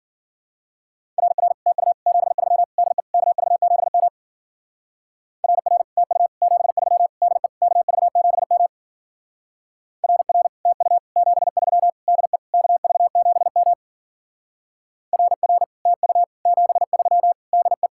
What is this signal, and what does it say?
Morse code, heard as a keyed tone, about 50 wpm, 700 Hz. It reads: RR TU 73 DE KV7M RR TU 73 DE KV7M RR TU 73 DE KV7M RR TU 73 DE
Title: Speed-Racing Common QSO Phrases Encore - 40wpm